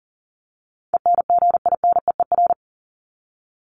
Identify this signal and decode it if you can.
Morse code, heard as a keyed tone, about 40 wpm, 700 Hz. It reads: ENGINEER
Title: ING Suffix - Top 100 Words with Base Words 40wpm